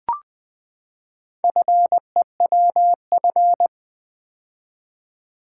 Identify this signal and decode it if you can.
Morse code, heard as a keyed tone, about 20 wpm, 700 Hz. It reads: FEWF
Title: Four Letter-Number Combinations - 20wpm